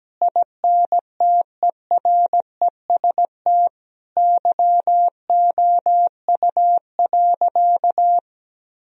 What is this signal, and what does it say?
Morse code, heard as a keyed tone, about 17 wpm, 700 Hz. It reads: INTEREST YOU.
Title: Sentences from Top 1000 Words - Review Part 4 - 17wpm